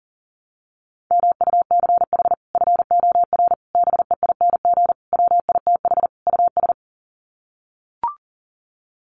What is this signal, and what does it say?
Morse code, heard as a keyed tone, about 40 wpm, 700 Hz. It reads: MUCH FOR BEING WITH US
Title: Sets of 5 Words 40wpm